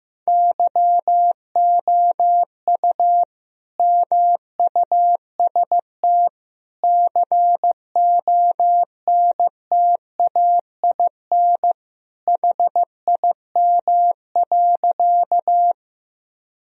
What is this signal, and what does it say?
Morse code, heard as a keyed tone, about 15 wpm, 700 Hz. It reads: YOU MUST CONTAIN HIM.
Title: Sentences from Top 1000 Words - Review Part 2 - 15wpm